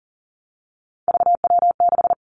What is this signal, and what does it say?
Morse code, heard as a keyed tone, about 40 wpm, 700 Hz. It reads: VW6
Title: Three Challenging Letter-Number Combinations 40wpm